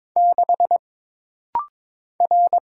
Morse code, heard as a keyed tone, about 22 wpm, 700 Hz. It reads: 6 R
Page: This rapid-fire timing is geared to push your ICR (Instant Character Recognition) skills.